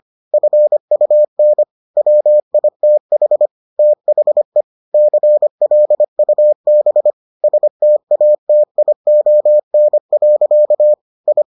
Morse code, heard as a keyed tone, about 25 wpm, 600 Hz. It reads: FUN WITH THE CLUB STATION. I